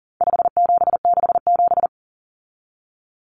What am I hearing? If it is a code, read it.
Morse code, heard as a keyed tone, about 40 wpm, 700 Hz. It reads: 5767
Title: Common Cut Numbers - 4-Digit 40wpm